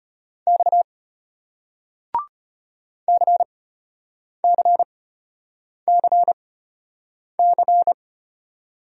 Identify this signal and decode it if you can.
Morse code, heard as a keyed tone, about 34 wpm, spaced as for 8 wpm, 700 Hz. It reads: X CCCC